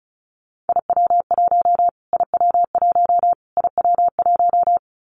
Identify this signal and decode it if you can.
Morse code, heard as a keyed tone, about 34 wpm, 700 Hz. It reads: IW1 IW1 IW1
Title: Callsign Prefixes for Top 25 DXCC - Repeated 3x - 35wpm